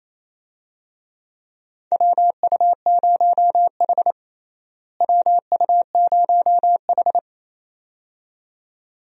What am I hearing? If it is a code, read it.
Morse code, heard as a keyed tone, about 28 wpm, 700 Hz. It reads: WU0H WU0H
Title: USA Call Signs - 4 Characters - Repeated 2x - 28wpm